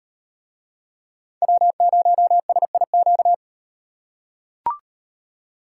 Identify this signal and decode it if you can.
Morse code, heard as a keyed tone, about 38 wpm, 700 Hz. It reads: W0SIQ